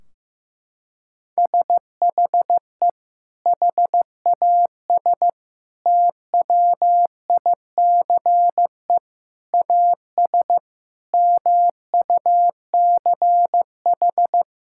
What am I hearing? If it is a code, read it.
Morse code, heard as a keyed tone, about 16 wpm, 700 Hz. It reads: SHE HAS TWICE AS MUCH
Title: Sentences from Top 1600 Words 15wpm